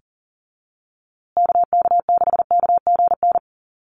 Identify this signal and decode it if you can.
Morse code, heard as a keyed tone, about 40 wpm, 700 Hz. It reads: KK6KGN
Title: USA Call Signs Repeated 3x 40wpm